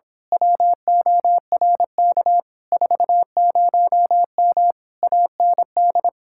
Morse code, heard as a keyed tone, about 26 wpm, 700 Hz. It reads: WORK 40M AND
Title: Speed-Racing Common QSO Phrases - 22wpm